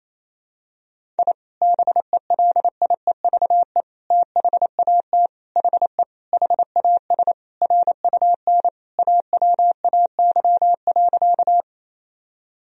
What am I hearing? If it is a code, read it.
Morse code, heard as a keyed tone, about 28 wpm, 700 Hz. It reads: I BELIEVE THAT HE HAS RUN AWAY.
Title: Sentences from Top 1000 Words - Review Part 1 - 28wpm